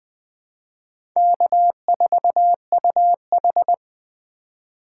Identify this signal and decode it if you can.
Morse code, heard as a keyed tone, about 20 wpm, 700 Hz. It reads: K4UH
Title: USA Call Signs - 2x Character Spacing - 20wpm